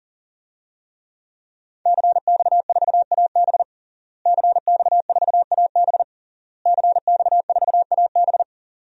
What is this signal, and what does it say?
Morse code, heard as a keyed tone, about 40 wpm, 700 Hz. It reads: CX4AB CX4AB CX4AB
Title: DXCC-100 Call Signs - 5 Characters - Repeated 3x - 40wpm